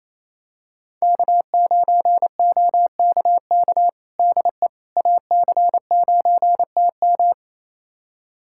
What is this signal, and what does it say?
Morse code, heard as a keyed tone, about 28 wpm, 700 Hz. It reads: K9OKK DE AC9TM
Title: USA Callsign DE Callsign - 5 Characters - 28wpm